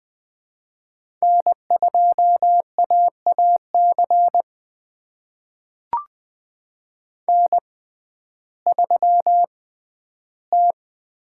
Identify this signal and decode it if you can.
Morse code, heard as a keyed tone, about 20 wpm, spaced as for 12 wpm, 700 Hz. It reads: N2AAC N3T
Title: USA Call Signs - 5 Characters - 6x Character Spacing - 20wpm